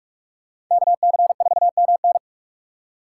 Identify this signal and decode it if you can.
Morse code, heard as a keyed tone, about 46 wpm, 700 Hz. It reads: KC4MN
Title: USA Call Signs - 5 Characters - 2x Character Spacing - 45wpm